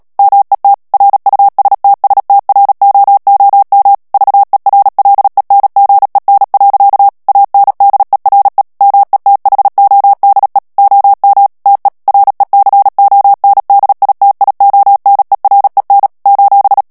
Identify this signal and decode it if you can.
Morse code, heard as a keyed tone, about 37 wpm, 800 Hz. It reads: MET RUSTSTROOM VERLENGEN. ANDERE METHODE OM TE RECONDITIONEREN :
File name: morse mp3 / HOLLANDAIS texte clair 2 vitesse 35 mots minute